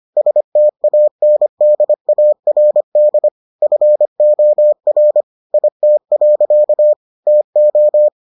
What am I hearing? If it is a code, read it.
Morse code, heard as a keyed tone, about 25 wpm, 600 Hz. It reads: STANDARD FOR IT. TO